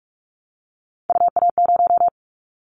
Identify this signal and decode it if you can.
Morse code, heard as a keyed tone, about 44 wpm, 700 Hz. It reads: UA0
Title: Callsign Prefixes for Top 100 DXCC - 45wpm